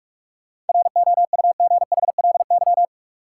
Morse code, heard as a keyed tone, about 46 wpm, 700 Hz. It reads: ROUGHLY
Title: Sentences from Top 1500 Words - New Word Drill - 45wpm